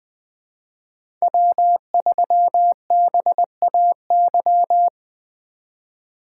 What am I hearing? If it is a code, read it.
Morse code, heard as a keyed tone, about 20 wpm, 700 Hz. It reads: W3BAY
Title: 2023 POTA Activator Call Signs - 20wpm